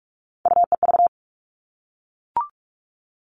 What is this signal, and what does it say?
Morse code, heard as a keyed tone, about 44 wpm, 700 Hz. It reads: UEV
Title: All Cut Numbers - 3-Digit 45wpm